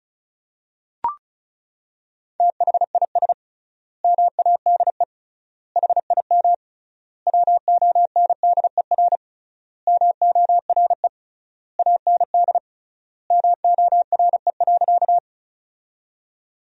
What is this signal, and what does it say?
Morse code, heard as a keyed tone, about 36 wpm, 700 Hz. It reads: THIS MADE HIM WONDER MORE AND MORE.
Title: 3x Word-Spacing - Sentences from Top 500 Words - 35wpm